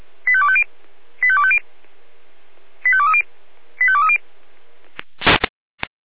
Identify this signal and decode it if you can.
Five-tone selective call (ZVEI), heard as quick runs of five tones, tones of about 70 ms. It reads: 85250 85250 85230 85230